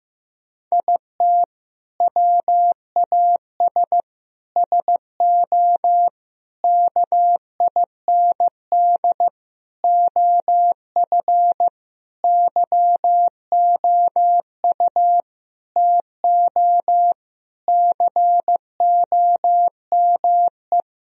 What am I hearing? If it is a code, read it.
Morse code, heard as a keyed tone, about 15 wpm, 700 Hz. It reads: IT WAS SO KIND OF YOU TO COME
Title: Sentences from Top 1000 Words - Review Part 3 - 15wpm